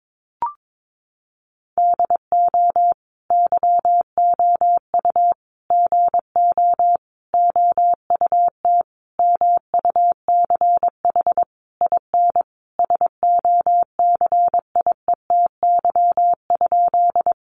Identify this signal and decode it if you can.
Morse code, heard as a keyed tone, about 22 wpm, 700 Hz. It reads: DO YOU GO OUT MUCH IN SOCIETY?
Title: Sentences from Top 700 Words 22wpm